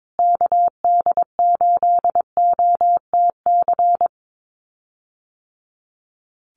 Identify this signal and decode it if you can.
Morse code, heard as a keyed tone, about 22 wpm, 700 Hz. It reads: KD8OTC
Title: USA Call Signs Repeated 3x 22wpm